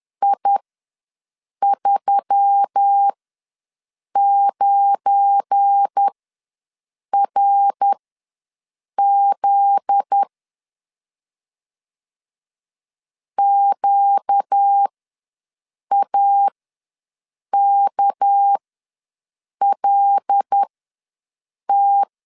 Morse code, heard as a keyed tone, about 10 wpm, spaced as for 6 wpm, 800 Hz. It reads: I39RZ QAKLT
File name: LEZIONI / 7 GF9Z / 7 GF9Z ws 10wpm 3919